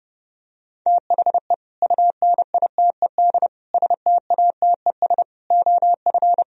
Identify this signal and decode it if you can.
Morse code, heard as a keyed tone, about 30 wpm, 700 Hz. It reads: THE UNITED STATES OF